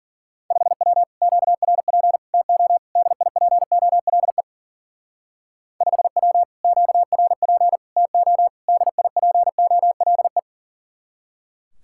Morse code, heard as a keyed tone, about 49 wpm, 700 Hz. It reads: 5W QRP TO DIPOLE 5W QRP TO DIPOLE
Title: Speed-Racing Common QSO Phrases - 40wpm